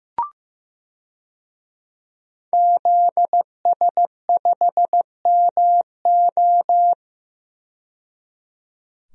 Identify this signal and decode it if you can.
Morse code, heard as a keyed tone, about 15 wpm, 700 Hz. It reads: ZS5MO
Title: DXCC-100 Call Signs - 5 Characters - 15wpm